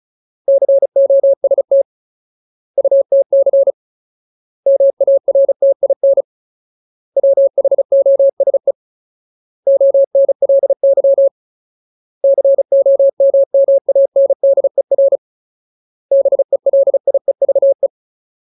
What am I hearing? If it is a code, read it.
Morse code, heard as a keyed tone, about 34 wpm, 550 Hz. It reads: COST UTC MARTIN WHOSE ONLY COMMANDER BELIEVE
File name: Ultimate Word List / 35WPM Ultimate